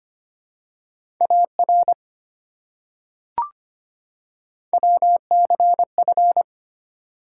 Morse code, heard as a keyed tone, about 25 wpm, 700 Hz. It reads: AR WCF
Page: morse-code-ninja-practice-25wpm